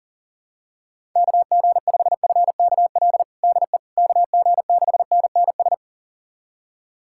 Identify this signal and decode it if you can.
Morse code, heard as a keyed tone, about 40 wpm, 700 Hz. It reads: KG5FKL DE KG6NNS
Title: USA Callsign DE Callsign 40wpm